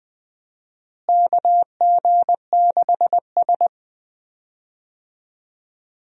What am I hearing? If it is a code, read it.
Morse code, heard as a keyed tone, about 20 wpm, 700 Hz. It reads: KG6S
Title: USA Call Signs - 4 Characters - Repeated 3x - 20wpm